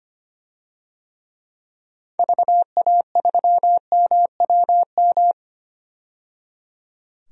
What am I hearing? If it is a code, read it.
Morse code, heard as a keyed tone, about 25 wpm, 700 Hz. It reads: VA3MWM
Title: DXCC-100 Call Signs - 6 Characters - 25wpm